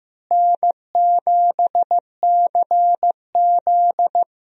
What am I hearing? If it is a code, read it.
Morse code, heard as a keyed tone, about 15 wpm, 700 Hz. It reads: N7CZ
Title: USA Call Signs - 4 Characters - Repeated 3x - 15wpm